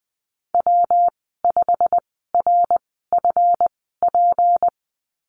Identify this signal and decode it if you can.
Morse code, heard as a keyed tone, about 20 wpm, spaced as for 15 wpm, 700 Hz. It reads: W5RFP
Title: 207 Lesson 33g - Letter j - Call Signs 20wpm